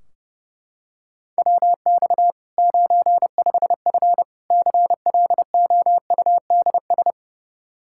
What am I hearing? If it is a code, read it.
Morse code, heard as a keyed tone, about 30 wpm, 700 Hz. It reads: WX 95F CLOUDS
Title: K1USN SST QSO Practice 30wpm